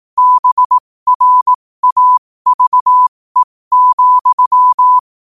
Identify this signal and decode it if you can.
Morse code, heard as a keyed tone, about 18 wpm, spaced as for 15 wpm, 1000 Hz. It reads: BRAVE,
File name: qotdfiles / 15wpm